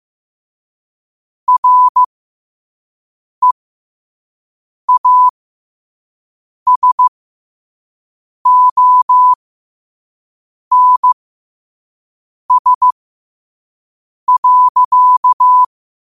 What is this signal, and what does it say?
Morse code, heard as a keyed tone, about 15 wpm, spaced as for 5 wpm, 1000 Hz. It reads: REASONS.
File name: qotdfiles / 5wpm